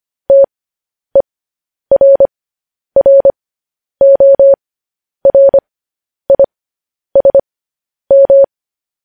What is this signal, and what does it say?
Morse code, heard as a keyed tone, about 25 wpm, spaced as for 10 wpm, 550 Hz. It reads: TERRORISM
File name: podcast / News headlines-1x-25x10